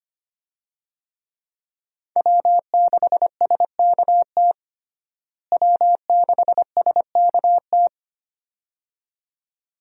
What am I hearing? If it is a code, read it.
Morse code, heard as a keyed tone, about 25 wpm, 700 Hz. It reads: W6SKT W6SKT
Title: USA Call Signs - 5 Characters - Repeated 2x - 25wpm